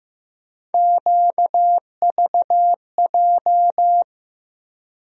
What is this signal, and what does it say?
Morse code, heard as a keyed tone, about 15 wpm, 700 Hz. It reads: QVJ
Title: USA Call Sign Postfixes - 3 Letters 15wpm